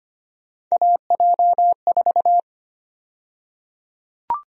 Morse code, heard as a keyed tone, about 25 wpm, 700 Hz. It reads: AJ4
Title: USA Call Sign Prefixes - 25wpm